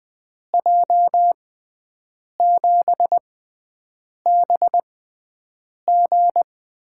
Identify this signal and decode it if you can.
Morse code, heard as a keyed tone, about 20 wpm, spaced as for 7 wpm, 700 Hz. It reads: J7BG
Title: USA Call Signs - 6 Characters - 6x Character Spacing - 20wpm